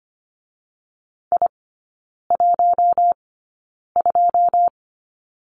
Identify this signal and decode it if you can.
Morse code, heard as a keyed tone, about 25 wpm, spaced as for 9 wpm, 700 Hz. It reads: I12
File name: Lesson 14e - Number 2 - Three Characters 25wpm